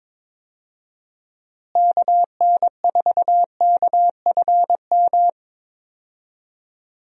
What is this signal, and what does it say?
Morse code, heard as a keyed tone, about 22 wpm, 700 Hz. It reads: KN4KFM
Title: USA Call Signs - 6 Characters - Repeated 3x - 22wpm